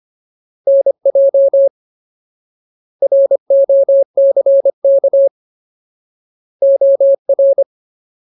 Morse code, heard as a keyed tone, about 25 wpm, 550 Hz. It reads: NJ ROCK OR
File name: Ultimate Word List / 25WPM Ultimate